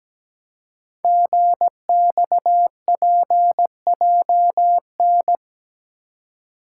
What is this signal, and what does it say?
Morse code, heard as a keyed tone, about 17 wpm, 700 Hz. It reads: GXPJN
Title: Code Groups - 5 Letter:Numbers - 17wpm